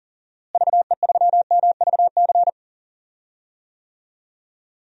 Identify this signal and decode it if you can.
Morse code, heard as a keyed tone, about 40 wpm, 700 Hz. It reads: VE3MVC
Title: DXCC-100 Call Signs - 6 Characters - Repeated 3x - 40wpm